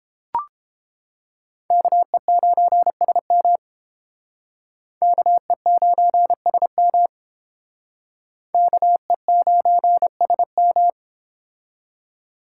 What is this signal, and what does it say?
Morse code, heard as a keyed tone, about 30 wpm, 700 Hz. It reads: KE9SM KE9SM KE9SM